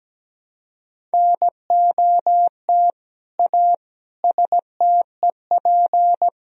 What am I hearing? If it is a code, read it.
Morse code, heard as a keyed tone, about 17 wpm, 700 Hz. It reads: NOT A STEP
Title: Sentences from Top 1000 Words - Review Part 2 - 17wpm